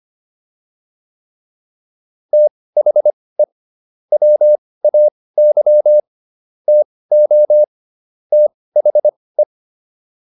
Morse code, heard as a keyed tone, about 25 wpm, spaced as for 18 wpm, 600 Hz. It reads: THE WAY TO THE
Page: intermediate-practice-files